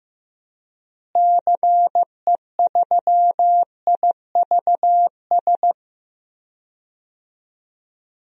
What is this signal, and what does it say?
Morse code, heard as a keyed tone, about 15 wpm, 700 Hz. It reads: CE3IVS
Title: DXCC-100 Call Signs - 6 Characters - Repeated 2x - 15wpm